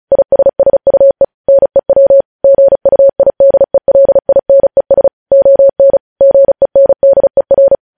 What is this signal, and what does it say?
Morse code, heard as a keyed tone, about 36 wpm, 550 Hz. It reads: ISSUE NEW GUIDELINES ON GENDER
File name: podcast / News headlines-1x-35x35